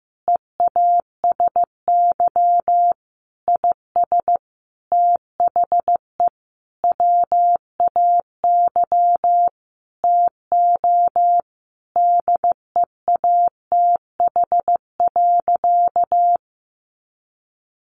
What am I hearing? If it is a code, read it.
Morse code, heard as a keyed tone, about 15 wpm, 700 Hz. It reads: EASY IS THE WAY TO DEATH.
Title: Sentences from Top 1400 Words 15wpm